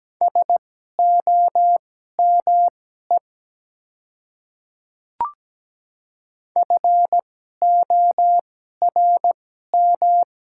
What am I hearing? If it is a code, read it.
Morse code, heard as a keyed tone, about 17 wpm, spaced as for 12 wpm, 690 Hz. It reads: SOME FORM
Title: 2x Character Spacing - Top 300 Words - 17wpm